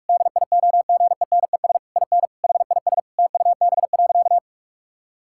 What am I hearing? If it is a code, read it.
Morse code, heard as a keyed tone, about 41 wpm, 700 Hz. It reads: DIOGENES IN HIS TUB.
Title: Pangram Challenge v2 - 45wpm